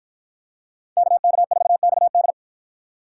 Morse code, heard as a keyed tone, about 53 wpm, 700 Hz. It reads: KK4XD